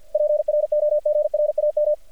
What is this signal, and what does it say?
Morse code, heard as a keyed tone, about 50 wpm, 600 Hz. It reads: PROGRAM
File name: archived / CRSnet